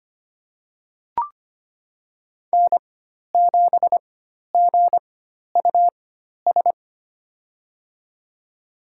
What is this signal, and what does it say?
Morse code, heard as a keyed tone, about 25 wpm, spaced as for 12 wpm, 700 Hz. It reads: N7GUS
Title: USA Call Signs - 4x Character Spacing - 25wpm